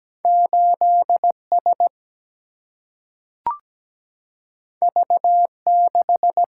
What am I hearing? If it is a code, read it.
Morse code, heard as a keyed tone, about 17 wpm, 700 Hz. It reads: 8S V6
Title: Single-Letter Single-Number 17wpm